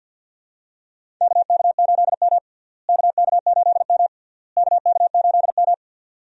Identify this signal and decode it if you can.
Morse code, heard as a keyed tone, about 50 wpm, 700 Hz. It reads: KK8M KK8M KK8M
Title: USA Call Signs - 4 Characters - Repeated 3x - 50wpm